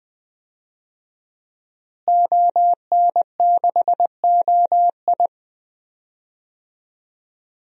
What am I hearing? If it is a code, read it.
Morse code, heard as a keyed tone, about 20 wpm, 700 Hz. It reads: ON6OI
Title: DXCC-100 Call Signs - 5 Characters - Repeated 2x - 20wpm